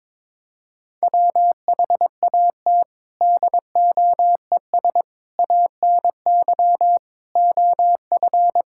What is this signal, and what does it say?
Morse code, heard as a keyed tone, about 22 wpm, 700 Hz. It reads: WHAT DOES ANY OF
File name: Sets of 9 Words - 22wpm